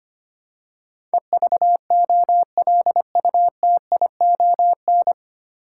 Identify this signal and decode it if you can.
Morse code, heard as a keyed tone, about 25 wpm, 700 Hz. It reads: EVOLUTION
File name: Nine Letter Words - 25wpm